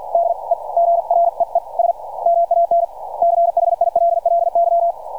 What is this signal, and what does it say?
Morse code, heard as a keyed tone, about 50 wpm, spaced as for 42 wpm, 680 Hz. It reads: DEGREES MAN CHILLY
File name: archived / CRSnet